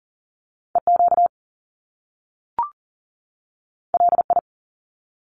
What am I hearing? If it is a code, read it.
Morse code, heard as a keyed tone, about 40 wpm, 700 Hz. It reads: EQ LI
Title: Top Two Letter Combinations in Words 40wpm